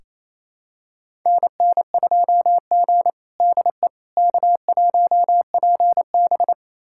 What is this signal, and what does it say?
Morse code, heard as a keyed tone, about 27 wpm, 700 Hz. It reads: NN2G DE K1PB
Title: USA Callsign DE Callsign - 4 Characters - 28wpm